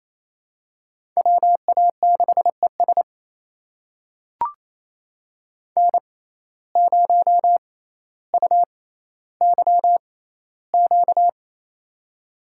Morse code, heard as a keyed tone, about 28 wpm, spaced as for 14 wpm, 700 Hz. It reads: WA6ES N0UYQ
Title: USA Call Signs - 5 Characters - 6x Character Spacing - 28wpm